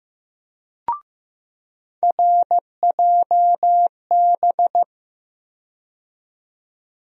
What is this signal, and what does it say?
Morse code, heard as a keyed tone, about 15 wpm, 700 Hz. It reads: RJB
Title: USA Call Sign Postfixes - 3 Letters 15wpm